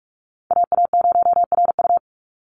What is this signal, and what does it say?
Morse code, heard as a keyed tone, about 46 wpm, 690 Hz. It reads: AA0RU
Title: USA Call Signs 45wpm